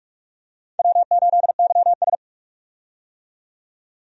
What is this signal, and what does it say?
Morse code, heard as a keyed tone, about 46 wpm, 700 Hz. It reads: W8YS
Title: USA Call Signs - 4 Characters - 45wpm